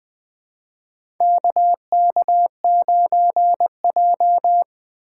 Morse code, heard as a keyed tone, about 20 wpm, 700 Hz. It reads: KK9J
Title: USA Call Signs - 4 Characters - 3x Character Spacing - 20wpm